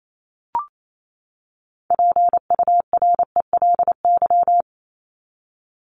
Morse code, heard as a keyed tone, about 28 wpm, 700 Hz. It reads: PURELY
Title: LY Suffix - Top 100 Words with Base Words 28wpm